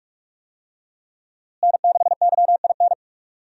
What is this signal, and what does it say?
Morse code, heard as a keyed tone, about 46 wpm, 700 Hz. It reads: N6YIN
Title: USA Call Signs - 5 Characters - Repeated 2x - 45wpm